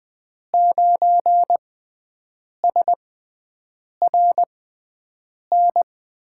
Morse code, heard as a keyed tone, about 20 wpm, spaced as for 7 wpm, 700 Hz. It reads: 9SRN
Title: USA Call Signs - 6 Characters - 6x Character Spacing - 20wpm